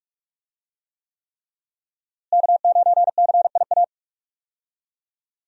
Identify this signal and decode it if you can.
Morse code, heard as a keyed tone, about 44 wpm, 700 Hz. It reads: K9CIA
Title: USA Call Signs - 5 Characters - Repeated 3x - 45wpm